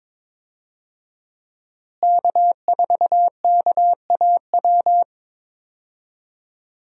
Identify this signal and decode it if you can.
Morse code, heard as a keyed tone, about 22 wpm, 700 Hz. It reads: K4KAW
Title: USA Call Signs - 5 Characters - Repeated 2x - 22wpm